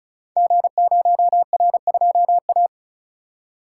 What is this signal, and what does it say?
Morse code, heard as a keyed tone, about 36 wpm, 690 Hz. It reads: G0R2A
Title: Code Groups - 5 Letter:Numbers - 35wpm